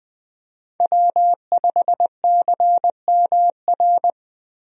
Morse code, heard as a keyed tone, about 21 wpm, 700 Hz. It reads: W5CMR
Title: USA Call Signs - 5 Characters - Repeated 3x - 20wpm